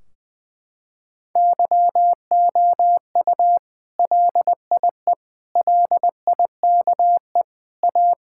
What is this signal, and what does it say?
Morse code, heard as a keyed tone, about 20 wpm, 700 Hz. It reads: YOU LIE LIKE A
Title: Sentences from Top 1600 Words 20wpm